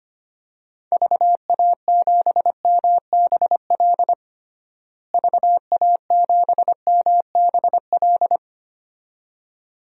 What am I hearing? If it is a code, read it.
Morse code, heard as a keyed tone, about 25 wpm, 700 Hz. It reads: VA7MBL VA7MBL
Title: DXCC-100 Call Signs - 6 Characters - Repeated 2x - 25wpm